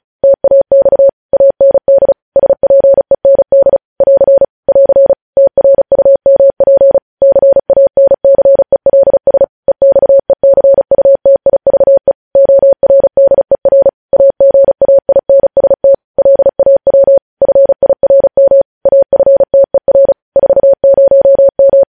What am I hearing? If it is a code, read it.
Morse code, heard as a keyed tone, about 36 wpm, 550 Hz. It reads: TAX AND SPEND <AR> <AR> TRUMP CANCELS EXECUTIVE ORDER AGAINST LAW FIRM AFTER 40M
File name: podcast / News headlines-1x-35x35